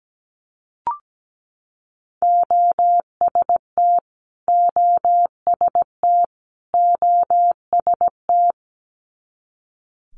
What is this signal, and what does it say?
Morse code, heard as a keyed tone, about 17 wpm, 690 Hz. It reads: OST OST OST
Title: Multi-letter Phonemes Repeated 3x 17wpm